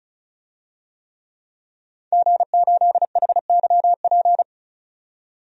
Morse code, heard as a keyed tone, about 36 wpm, 700 Hz. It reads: G8HYP
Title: DXCC-100 Call Signs - 5 Characters - 35wpm